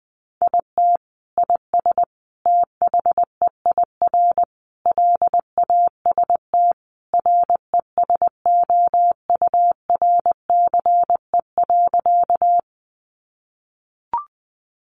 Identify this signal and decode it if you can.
Morse code, heard as a keyed tone, about 20 wpm, 700 Hz. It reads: IT IS THEIR LAST RESOURCE.
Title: Sentences from Top 1300 Words 20wpm